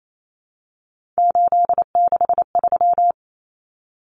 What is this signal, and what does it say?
Morse code, heard as a keyed tone, about 28 wpm, 700 Hz. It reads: 863
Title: Common Cut Numbers - 3-Digit 28wpm